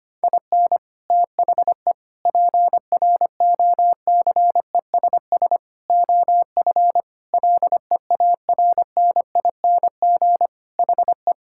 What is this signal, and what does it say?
Morse code, heard as a keyed tone, about 25 wpm, 700 Hz. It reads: IN THE PROCESS OF LEARNING HE
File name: practice / 25wpm / Sentences with 4-Word Phrases - 25wpm